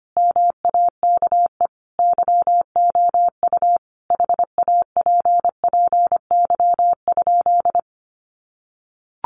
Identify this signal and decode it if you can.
Morse code, heard as a keyed tone, about 25 wpm, 700 Hz. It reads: MAKE YOU HAPPY?
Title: Sentences from Top 1000 Words 25wpm